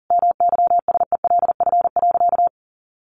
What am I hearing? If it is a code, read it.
Morse code, heard as a keyed tone, about 37 wpm, 700 Hz. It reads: MYSELF.
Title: Sentences from Top 1300 Words 40wpm